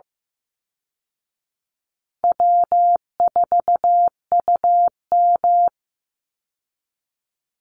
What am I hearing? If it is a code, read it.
Morse code, heard as a keyed tone, about 15 wpm, 700 Hz. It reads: W4UM
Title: Easy Call Signs Repeated 3x - 15wpm